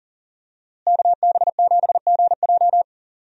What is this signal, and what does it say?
Morse code, heard as a keyed tone, about 37 wpm, 700 Hz. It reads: KB7GJ
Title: USA Call Signs - 5 Characters - Repeated 3x - 40wpm